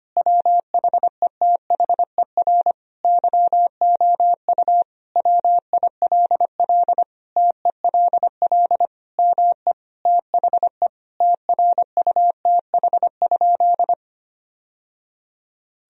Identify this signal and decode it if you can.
Morse code, heard as a keyed tone, about 25 wpm, 700 Hz. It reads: WHETHER YOU WILL TELL ME THE TRUTH?
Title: Sentences from Top 1000 Words - Review Part 3 - 25wpm